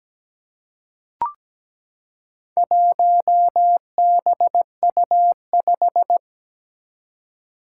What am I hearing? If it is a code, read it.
Morse code, heard as a keyed tone, about 17 wpm, 700 Hz. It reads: 1BU5
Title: Four Challenging Letter-Number Combinations 17wpm